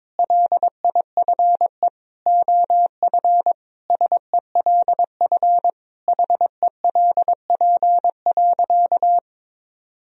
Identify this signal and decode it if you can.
Morse code, heard as a keyed tone, about 22 wpm, 700 Hz. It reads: LIFE OF SELF HELP.
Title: Sentences from Top 1000 Words - Review Part 1 - 22wpm